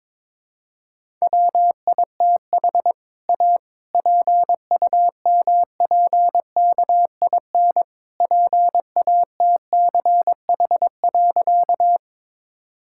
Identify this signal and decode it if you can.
Morse code, heard as a keyed tone, about 22 wpm, 700 Hz. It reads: WITH A PUMPKIN PATCH.
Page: morse-code-ninja-practice-22wpm